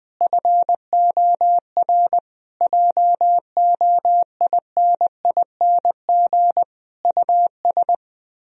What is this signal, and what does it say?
Morse code, heard as a keyed tone, about 20 wpm, 700 Hz. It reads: FOR JOINING US
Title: Speed-Racing - Sets of 5 Words 17wpm